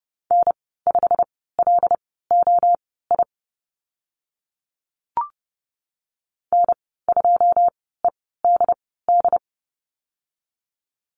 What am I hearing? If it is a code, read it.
Morse code, heard as a keyed tone, about 32 wpm, spaced as for 17 wpm, 700 Hz. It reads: N5LOI N2EDD
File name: Lesson 15g - Number 5 - Call Signs 30wpm